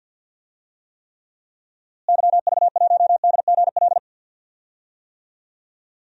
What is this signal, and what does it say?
Morse code, heard as a keyed tone, about 51 wpm, 700 Hz. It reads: YV1DGL
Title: DXCC-100 Call Signs - 6 Characters - Repeated 3x - 50wpm